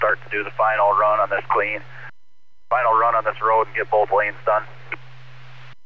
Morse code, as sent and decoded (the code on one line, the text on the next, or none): none